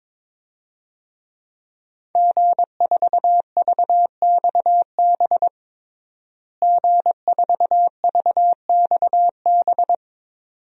Morse code, heard as a keyed tone, about 22 wpm, 700 Hz. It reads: G4VXB G4VXB
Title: DXCC-100 Call Signs - 5 Characters - Repeated 2x - 22wpm